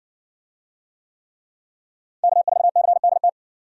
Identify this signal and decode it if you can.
Morse code, heard as a keyed tone, about 60 wpm, 700 Hz. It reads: K4CDT